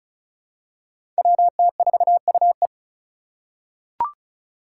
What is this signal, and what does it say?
Morse code, heard as a keyed tone, about 36 wpm, 700 Hz. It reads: WT4UE
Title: Five Challenging Letter-Number Combinations 35wpm